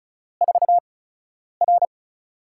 Morse code, heard as a keyed tone, about 36 wpm, spaced as for 10 wpm, 700 Hz. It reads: 4R
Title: USA Call Signs - 8x Character Spacing - 35wpm